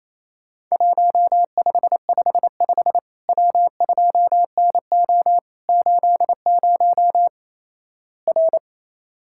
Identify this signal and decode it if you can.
Morse code, heard as a keyed tone, about 28 wpm, 700 Hz. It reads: 1555 W2NO 80 R
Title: Worked All Europe DX Contest - QTC Exchange - 28wpm